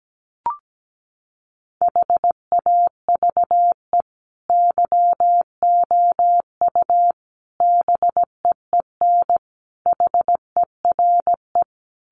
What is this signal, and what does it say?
Morse code, heard as a keyed tone, about 17 wpm, 700 Hz. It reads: HAVE YOU BEEN HERE
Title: Sentences from Top 200 Words 17wpm